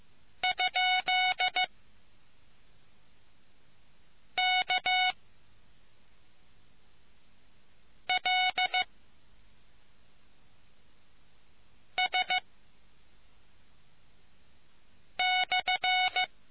Morse code, heard as a keyed tone, about 15 wpm, spaced as for 3 wpm, 740 Hz. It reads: ?KLS/